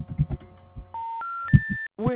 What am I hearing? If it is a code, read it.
Special information tone, with speech.
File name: recordings / REb2428d708db574d2a1c05448dd88c54f